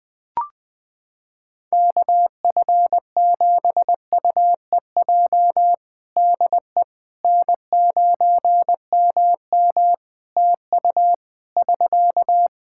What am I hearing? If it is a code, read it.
Morse code, heard as a keyed tone, about 20 wpm, 700 Hz. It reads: KF7UEJ DE N9MM TU <SK>
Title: Practice QSO - Exchange 4 - Ending - 20wpm